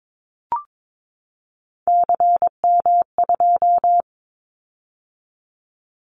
Morse code, heard as a keyed tone, about 22 wpm, 700 Hz. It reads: CM2
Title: Callsign Prefixes for Top 100 DXCC - 22wpm